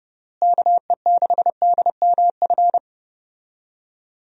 Morse code, heard as a keyed tone, about 30 wpm, 700 Hz. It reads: KE6DMF
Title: USA Call Signs - 6 Characters - Repeated 3x - 30wpm